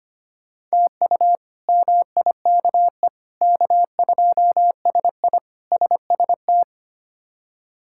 Morse code, heard as a keyed tone, about 25 wpm, 700 Hz. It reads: TU MIKE K2SI SST
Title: K1USN SST QSO Practice - 25wpm